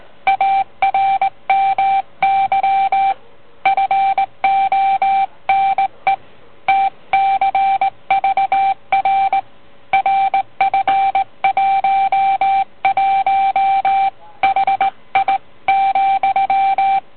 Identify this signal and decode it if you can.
Morse code, heard as a keyed tone, about 18 wpm, 780 Hz. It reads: ARMY FONE TCVR RF11 HI ,